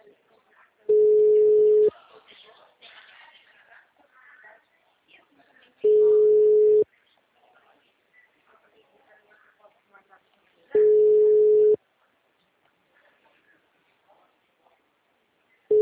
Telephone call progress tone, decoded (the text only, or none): ringback tone